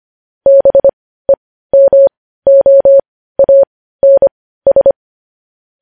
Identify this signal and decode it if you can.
Morse code, heard as a keyed tone, about 25 wpm, spaced as for 15 wpm, 550 Hz. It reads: BEMOANS
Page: News Headlines in Morse Code at 15 WPM